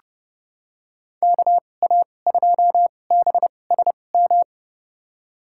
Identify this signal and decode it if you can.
Morse code, heard as a keyed tone, about 30 wpm, spaced as for 22 wpm, 700 Hz. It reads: KA2BSM
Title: USA Call Signs - 2x Character Spacing - 30wpm